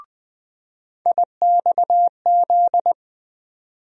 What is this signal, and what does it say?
Morse code, heard as a keyed tone, about 20 wpm, 700 Hz. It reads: IXZ
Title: USA Call Sign Postfixes - 3 Letters 20wpm